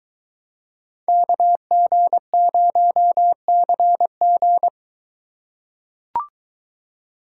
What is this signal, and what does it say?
Morse code, heard as a keyed tone, about 23 wpm, 700 Hz. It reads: KG0CG